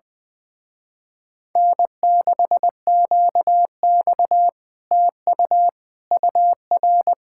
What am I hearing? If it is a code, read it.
Morse code, heard as a keyed tone, about 20 wpm, 700 Hz. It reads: N6QX TU UR
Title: Speed-Racing Common QSO Phrases - 20wpm